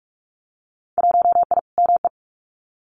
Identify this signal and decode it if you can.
Morse code, heard as a keyed tone, about 46 wpm, 700 Hz. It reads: 1I ME
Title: Winter Field Day Exchanges 45wpm